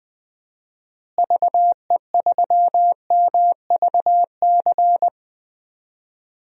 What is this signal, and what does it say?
Morse code, heard as a keyed tone, about 20 wpm, 700 Hz. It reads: VE3MVC
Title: DXCC-100 Call Signs - 6 Characters - Repeated 3x - 20wpm